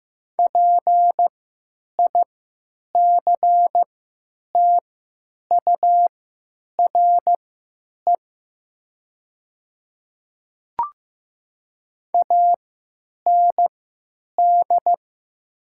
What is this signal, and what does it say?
Morse code, heard as a keyed tone, about 15 wpm, spaced as for 9 wpm, 700 Hz. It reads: PICTURE AND
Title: 3x Character Spacing - Top 300 Words - 15wpm